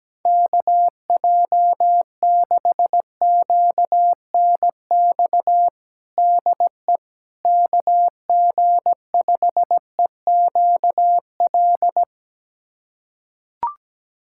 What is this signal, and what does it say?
Morse code, heard as a keyed tone, about 17 wpm, 700 Hz. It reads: KJ6QNX DE KG5EQL
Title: USA Callsign DE Callsign - 6 Characters - 17wpm